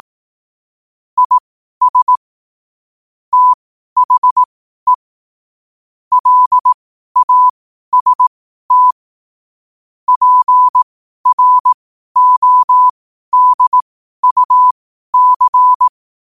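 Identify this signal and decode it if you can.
Morse code, heard as a keyed tone, about 18 wpm, spaced as for 12 wpm, 1000 Hz. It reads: IS THE LAST PRODUC
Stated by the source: Morse code at 12 wpm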